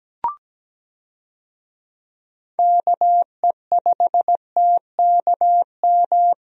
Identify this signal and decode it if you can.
Morse code, heard as a keyed tone, about 17 wpm, 700 Hz. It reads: KE5TKM
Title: USA Call Signs - 6 Characters - 17wpm